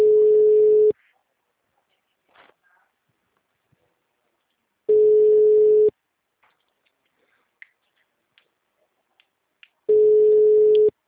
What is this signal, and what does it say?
Ringback tone (Europe, 425 Hz).
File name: out- -2005- - - .97970